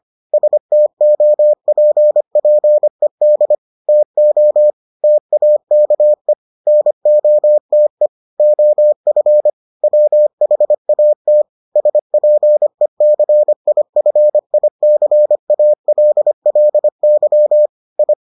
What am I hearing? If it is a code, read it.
Morse code, heard as a keyed tone, about 25 wpm, 600 Hz. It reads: STOPPED TO TAKE NOTE OF WHAT SPECIFICALLY I